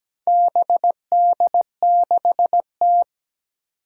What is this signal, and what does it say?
Morse code, heard as a keyed tone, about 17 wpm, 700 Hz. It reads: BD6T
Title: Four Challenging Letter-Number Combinations 17wpm